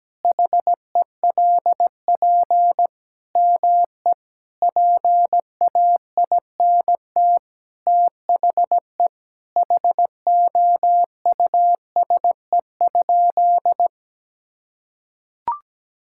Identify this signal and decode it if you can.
Morse code, heard as a keyed tone, about 17 wpm, 700 Hz. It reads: HELP ME PAINT THE HOUSE?
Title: Sentences from Top 1000 Words - Review Part 6 - 17wpm